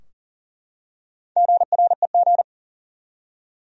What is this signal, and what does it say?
Morse code, heard as a keyed tone, about 40 wpm, 700 Hz. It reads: GREG
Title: USA Common Names - Max 4 Letters - 40wpm